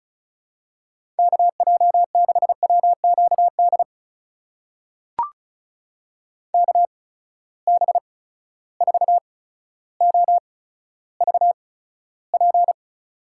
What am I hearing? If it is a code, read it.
Morse code, heard as a keyed tone, about 36 wpm, spaced as for 15 wpm, 700 Hz. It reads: KJ6WQD KB4OVP
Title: USA Call Signs - 6 Characters - 8x Character Spacing - 35wpm